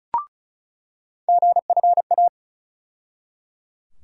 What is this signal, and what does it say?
Morse code, heard as a keyed tone, about 34 wpm, 700 Hz. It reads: GFA
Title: USA Call Sign Postfixes - 3 Letters 35wpm